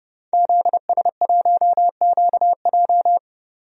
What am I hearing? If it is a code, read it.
Morse code, heard as a keyed tone, about 31 wpm, 700 Hz. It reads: ZS1QJ
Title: DXCC-100 Call Signs - 5 Characters - 30wpm